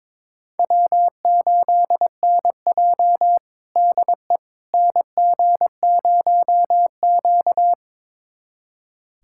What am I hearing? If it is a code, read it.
Morse code, heard as a keyed tone, about 22 wpm, 700 Hz. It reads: W8NJ DE NG0Q
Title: USA Callsign DE Callsign - 4 Characters - 22wpm